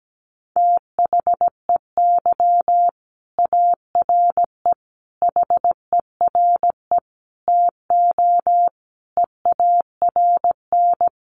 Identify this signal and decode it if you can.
Morse code, heard as a keyed tone, about 17 wpm, 700 Hz. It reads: THEY ARE HERE TO EARN
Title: Sentences from Top 1500 Words 17wpm